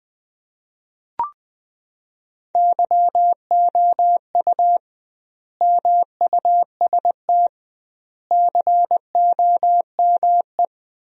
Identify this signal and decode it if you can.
Morse code, heard as a keyed tone, about 21 wpm, 700 Hz. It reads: YOU MUST COME
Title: 2x Word-Spacing - Sentences from Top 200 Words - 20wpm